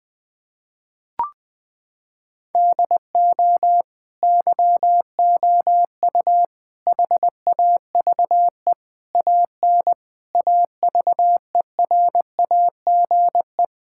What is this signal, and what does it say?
Morse code, heard as a keyed tone, about 21 wpm, 700 Hz. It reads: DO YOU HAVE AN AVERAGE
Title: Sentences with 9-Word Phrases - 20wpm